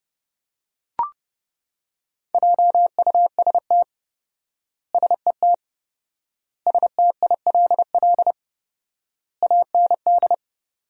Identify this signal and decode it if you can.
Morse code, heard as a keyed tone, about 30 wpm, 700 Hz. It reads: JUST SET STILL AND
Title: 4x Word-Spacing - Sentences from Top 300 Words - 30wpm